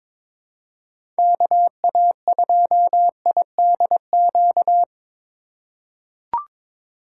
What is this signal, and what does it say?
Morse code, heard as a keyed tone, about 23 wpm, 700 Hz. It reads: KA2IDQ
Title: USA Call Signs - 4x Character Spacing - 22wpm